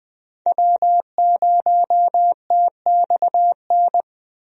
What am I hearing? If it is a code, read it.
Morse code, heard as a keyed tone, about 20 wpm, 700 Hz. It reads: W0TXN
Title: USA Call Signs - 5 Characters - Repeated 3x - 20wpm